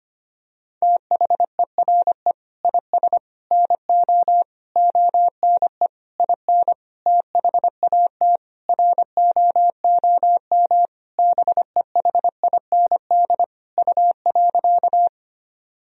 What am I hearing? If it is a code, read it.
Morse code, heard as a keyed tone, about 25 wpm, 700 Hz. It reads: THERE IS NO ONE IN THAT ROOM BEHIND U.
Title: Sentences from Top 500 Words and Abbreviations 25wpm